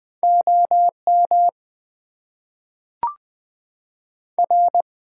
Morse code, heard as a keyed tone, about 20 wpm, 700 Hz. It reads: OM R
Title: Top 90 QSO Elements - 8x Character Spacing - 20wpm